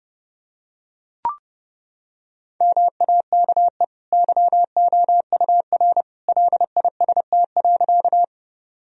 Morse code, heard as a keyed tone, about 28 wpm, 700 Hz. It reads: MAKE YOUR LIST.
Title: Sentences from Top 1000 Words - Review Part 2 - 30wpm